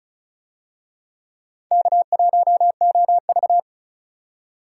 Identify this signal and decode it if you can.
Morse code, heard as a keyed tone, about 36 wpm, 700 Hz. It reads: K1OV
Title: USA Call Signs - 4 Characters - 35wpm